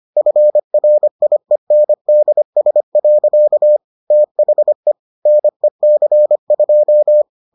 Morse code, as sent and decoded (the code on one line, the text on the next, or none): ..-. .-. .. -. -.. ... .-.-.- / - .... / -. -.-. ..---
FRIENDS. THE NEC2